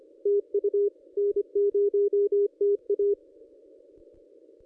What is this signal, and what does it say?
Morse code, heard as a keyed tone, about 24 wpm, 410 Hz. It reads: TU N0TA
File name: contests / ss-cw-k5tr-op-n5rz -1500